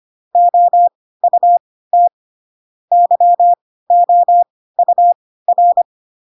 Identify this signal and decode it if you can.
Morse code, heard as a keyed tone, about 25 wpm, spaced as for 16 wpm, 700 Hz. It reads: OUT YOUR
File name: Pinocchio / Pinocchio ch6 10 25 ik1hge